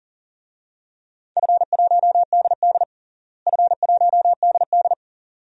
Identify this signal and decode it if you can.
Morse code, heard as a keyed tone, about 40 wpm, 700 Hz. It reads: F1DD F1DD
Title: DXCC-100 Call Signs - 4 Characters - Repeated 2x - 40wpm